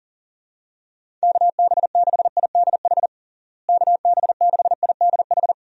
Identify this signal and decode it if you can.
Morse code, heard as a keyed tone, about 40 wpm, 700 Hz. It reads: KB6IDH KB6IDH
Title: USA Call Signs - 6 Characters - Repeated 2x - 40wpm